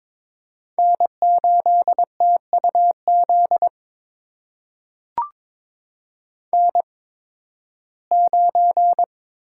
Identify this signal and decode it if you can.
Morse code, heard as a keyed tone, about 22 wpm, spaced as for 14 wpm, 700 Hz. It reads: N8TUZ N9
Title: USA Call Signs - 5 Characters - 8x Character Spacing - 22wpm